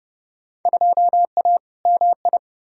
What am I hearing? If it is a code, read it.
Morse code, heard as a keyed tone, about 30 wpm, 700 Hz. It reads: 2A MI
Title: Field Day Exchanges - 30wpm